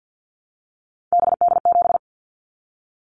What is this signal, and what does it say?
Morse code, heard as a keyed tone, about 50 wpm, 700 Hz. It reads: BD7
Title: Callsign Prefixes for Top 100 DXCC - 50wpm